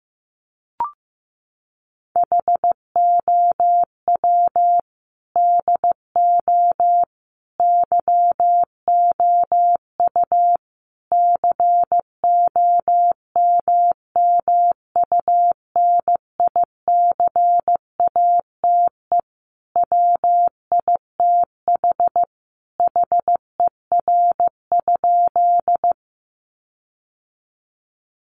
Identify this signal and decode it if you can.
Morse code, heard as a keyed tone, about 15 wpm, 700 Hz. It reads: HOW DO YOU COMMUNICATE WITH HER?
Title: Sentences from Top 1300 Words 15wpm